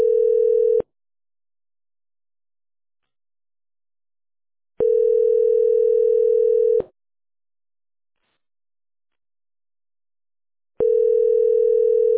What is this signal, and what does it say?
Ringback tone (North America), 440 + 480 Hz.